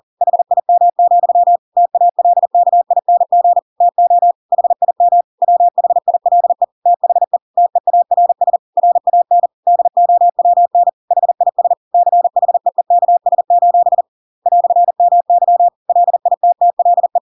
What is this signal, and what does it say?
Morse code, heard as a keyed tone, about 40 wpm, 700 Hz. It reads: HIM, TALKING TO HIM WHILE THE TEARS RAN DOWN HIS CHEEKS: "MY LITTLE
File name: Pinocchio / Pinocchio ch7 40 40 ik1hge